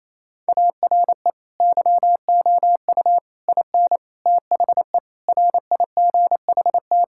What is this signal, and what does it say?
Morse code, heard as a keyed tone, about 28 wpm, 700 Hz. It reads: ARE YOU IN THE RIGHT
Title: Sentences from Top 1000 Words - Review Part 6 - 28wpm